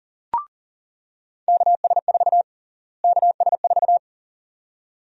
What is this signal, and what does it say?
Morse code, heard as a keyed tone, about 40 wpm, 700 Hz. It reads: KS4 KS4
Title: USA Call Sign Prefixes - Repeated 2x - 40wpm